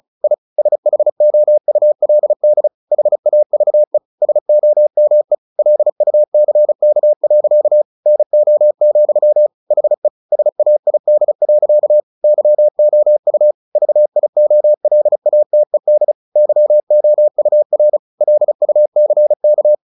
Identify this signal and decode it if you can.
Morse code, heard as a keyed tone, about 34 wpm, 600 Hz. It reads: I SHOULD HAVE SOME LUCK. NO, HE SAID. YOU VIOLATED YOUR LUCK
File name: the old man and the sea 2 35wpm